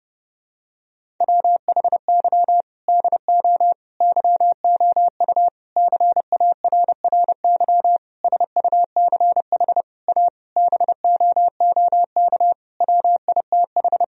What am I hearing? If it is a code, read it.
Morse code, heard as a keyed tone, about 30 wpm, 700 Hz. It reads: WHY DO YOU CARRY SUCH A BOOK WITH
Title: Sentences from Top 1000 Words - Review Part 1 - 30wpm